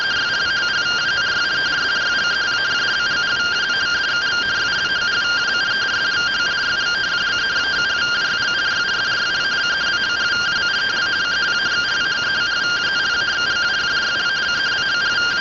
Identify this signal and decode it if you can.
Two-tone teleprinter signal, about 170 Hz shift.